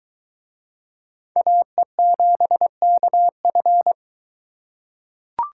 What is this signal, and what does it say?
Morse code, heard as a keyed tone, about 23 wpm, 700 Hz. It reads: AE7KF